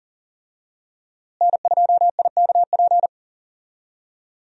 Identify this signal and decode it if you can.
Morse code, heard as a keyed tone, about 40 wpm, 700 Hz. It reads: N2IKP
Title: USA Call Signs - 5 Characters - Repeated 3x - 40wpm